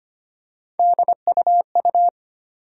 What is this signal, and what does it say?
Morse code, heard as a keyed tone, about 25 wpm, 700 Hz. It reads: DUU
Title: USA Call Sign Postfixes - 3 Letters 25wpm